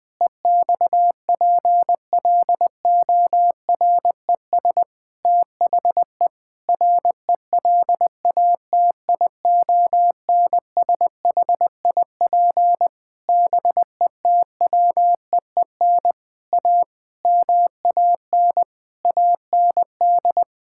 Morse code, heard as a keyed tone, about 20 wpm, 700 Hz. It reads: EXPLORES THE RELATIONSHIP BETWEEN A MAN AND
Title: Sentences with 6-Word Phrases - 20wpm